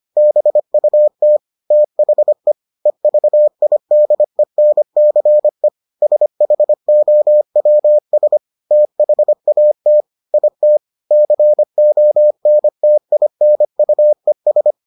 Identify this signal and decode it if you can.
Morse code, heard as a keyed tone, about 25 wpm, 600 Hz. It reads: BUT THE EVIDENCE SHOWS THAT IT CONTINUES